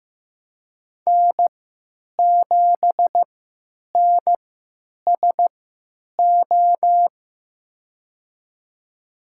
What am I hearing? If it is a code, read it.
Morse code, heard as a keyed tone, about 15 wpm, spaced as for 9 wpm, 700 Hz. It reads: N7NSO
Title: USA Call Signs - 5 Characters - 3x Character Spacing - 15wpm